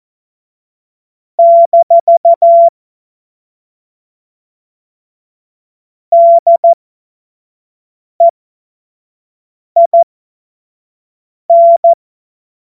Morse code, heard as a keyed tone, about 14 wpm, spaced as for 5 wpm, 680 Hz. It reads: - DEIN
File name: Morsekurs / Witze / Witz 26. Oktober 2021, 25 F70